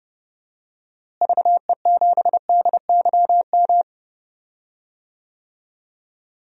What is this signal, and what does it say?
Morse code, heard as a keyed tone, about 31 wpm, 700 Hz. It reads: VE7DYM
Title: DXCC-100 Call Signs - 6 Characters - Repeated 2x - 30wpm